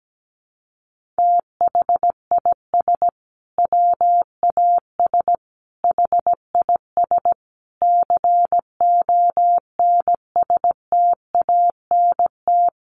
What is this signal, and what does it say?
Morse code, heard as a keyed tone, about 17 wpm, 700 Hz. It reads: THIS WAS HIS CONSTANT
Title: Sentences from Top 1500 Words 17wpm